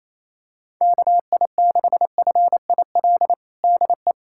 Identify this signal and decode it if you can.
Morse code, heard as a keyed tone, about 28 wpm, 700 Hz. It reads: KI6FIL DE
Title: USA Callsign DE Callsign 28wpm